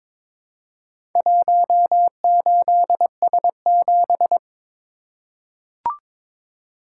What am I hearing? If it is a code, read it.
Morse code, heard as a keyed tone, about 22 wpm, 700 Hz. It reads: 18S7
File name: practice / 22wpm / Four Letter-Number Combinations - 22wpm